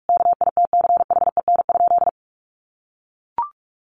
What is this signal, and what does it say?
Morse code, heard as a keyed tone, about 44 wpm, 700 Hz. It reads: KITCHEN?
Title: Sentences from Top 1400 Words 45wpm